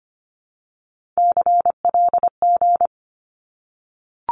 Morse code, heard as a keyed tone, about 25 wpm, 700 Hz. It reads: CLG
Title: Abbreviations Repeated 3x - 25wpm